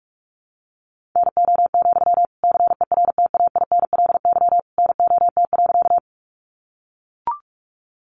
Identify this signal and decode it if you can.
Morse code, heard as a keyed tone, about 44 wpm, 700 Hz. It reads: NO, CERTAINLY NOT.
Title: Sentences from Top 900 Words 45wpm